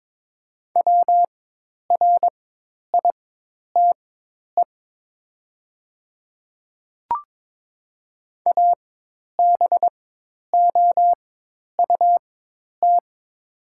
Morse code, heard as a keyed tone, about 22 wpm, spaced as for 10 wpm, 700 Hz. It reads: WRITE ABOUT
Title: 4x Character Spacing - Top 100 Words - 22wpm